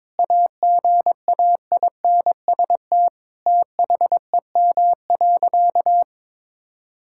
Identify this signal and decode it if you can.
Morse code, heard as a keyed tone, about 22 wpm, 700 Hz. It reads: AGAINST THEM.
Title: Sentences from Top 1000 Words - Review Part 6 - 22wpm